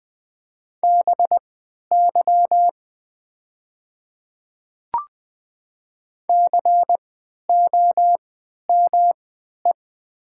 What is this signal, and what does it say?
Morse code, heard as a keyed tone, about 20 wpm, spaced as for 11 wpm, 700 Hz. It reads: BY COME
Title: 3x Character Spacing - Top 200 Words - 20wpm